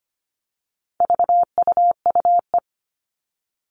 Morse code, heard as a keyed tone, about 25 wpm, 700 Hz. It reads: VUUE
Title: All Cut Numbers - 4-Digit 25wpm